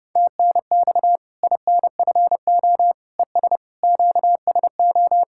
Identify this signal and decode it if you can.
Morse code, heard as a keyed tone, about 30 wpm, 700 Hz. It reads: TNX INFO ES QSO
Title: K1USN SST QSO Practice 30wpm